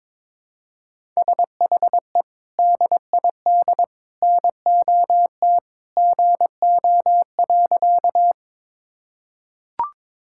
Morse code, heard as a keyed tone, about 22 wpm, 700 Hz. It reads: SHE DID NOT GO.
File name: practice / 22wpm / Sentences with 3-Word Phrases - 22wpm